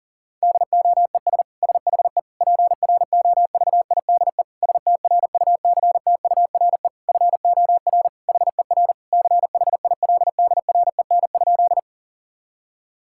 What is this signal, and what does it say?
Morse code, heard as a keyed tone, about 40 wpm, 700 Hz. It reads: DOES SHE PROVIDE STRUCTURE FOR HER CHILDREN?
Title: Sentences from Top 1000 Words - Review Part 6 - 40wpm